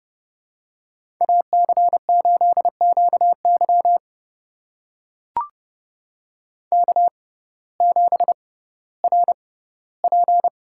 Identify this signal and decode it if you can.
Morse code, heard as a keyed tone, about 30 wpm, spaced as for 17 wpm, 700 Hz. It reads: AC8QY K7RP
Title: USA Call Signs - 5 Characters - 6x Character Spacing - 30wpm